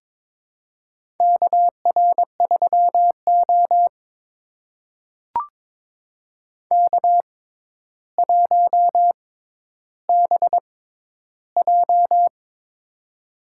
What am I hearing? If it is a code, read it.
Morse code, heard as a keyed tone, about 22 wpm, spaced as for 11 wpm, 700 Hz. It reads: KR3O K1BJ
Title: USA Call Signs - 4 Characters - 6x Character Spacing - 22wpm